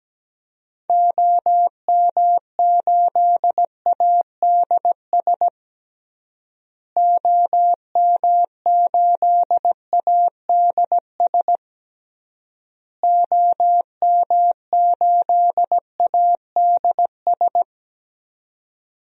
Morse code, heard as a keyed tone, about 17 wpm, 700 Hz. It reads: OM8ADS OM8ADS OM8ADS
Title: DXCC-100 Call Signs - 6 Characters - Repeated 3x - 17wpm